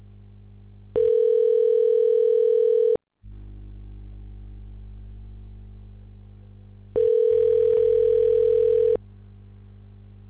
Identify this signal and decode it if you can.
Ringback tone (North America), 440 + 480 Hz.
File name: recordings / RE5ab c5464f24bbf986d05528b